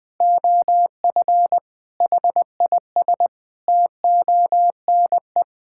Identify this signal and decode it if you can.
Morse code, heard as a keyed tone, about 20 wpm, 700 Hz. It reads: OF HIS TONE
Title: Sentences from Top 1600 Words 20wpm